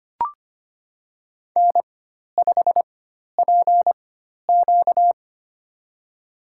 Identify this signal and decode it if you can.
Morse code, heard as a keyed tone, about 25 wpm, spaced as for 12 wpm, 700 Hz. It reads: N5PQ
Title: USA Call Signs - 4x Character Spacing - 25wpm